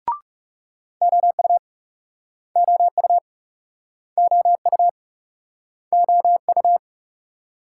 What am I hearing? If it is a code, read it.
Morse code, heard as a keyed tone, about 38 wpm, 700 Hz. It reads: OU OU OU OU